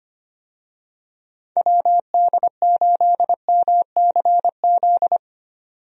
Morse code, heard as a keyed tone, about 25 wpm, 700 Hz. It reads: WD8MCZ
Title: USA Call Signs - 6 Characters - 25wpm